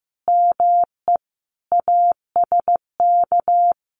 Morse code, heard as a keyed tone, about 15 wpm, 700 Hz. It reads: ME ASK
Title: Sets of 5 Words 15wpm